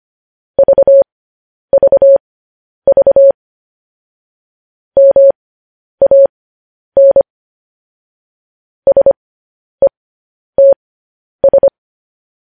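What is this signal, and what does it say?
Morse code, heard as a keyed tone, about 25 wpm, spaced as for 10 wpm, 550 Hz. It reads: VVV MAN SETS
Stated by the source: News headlines sent in Morse code at 10 wpm.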